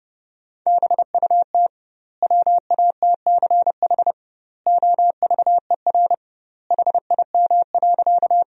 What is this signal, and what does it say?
Morse code, heard as a keyed tone, about 30 wpm, 700 Hz. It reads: BUT WATCH OVER HIM.
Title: 2x Word-Spacing - Sentences from Top 300 Words - 30wpm